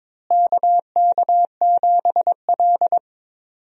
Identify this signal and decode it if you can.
Morse code, heard as a keyed tone, about 22 wpm, 700 Hz. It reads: KK7L
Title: USA Call Signs - 4 Characters - Repeated 3x - 22wpm